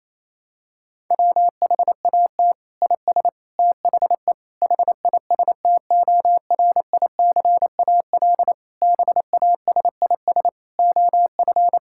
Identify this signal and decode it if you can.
Morse code, heard as a keyed tone, about 27 wpm, 700 Hz. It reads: WHAT IS THE HISTORICAL BASIS OF
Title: Sentences from Top 1000 Words - Review Part 5 - 28wpm